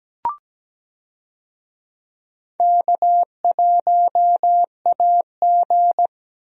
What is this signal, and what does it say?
Morse code, heard as a keyed tone, about 17 wpm, 700 Hz. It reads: K1AG
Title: USA Call Signs - 4 Characters - Repeated 3x - 17wpm